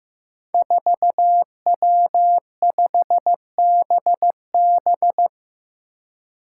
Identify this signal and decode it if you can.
Morse code, heard as a keyed tone, about 15 wpm, 700 Hz. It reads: 4W5BB
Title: Five Challenging Letter-Number Combinations 15wpm